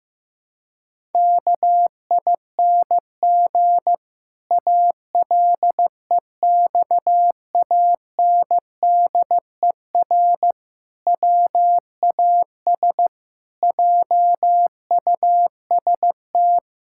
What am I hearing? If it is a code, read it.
Morse code, heard as a keyed tone, about 15 wpm, 700 Hz. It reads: KING ALEXANDER WAS JUST
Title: Pangram Challenge v2 - 15wpm